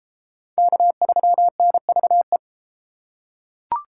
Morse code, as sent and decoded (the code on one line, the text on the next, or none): -.- ...-- -. ...- .
K3NVE